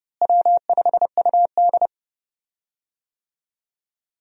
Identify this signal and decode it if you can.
Morse code, heard as a keyed tone, about 30 wpm, 700 Hz. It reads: W5UD
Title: USA Call Signs - 4 Characters - Repeated 3x - 30wpm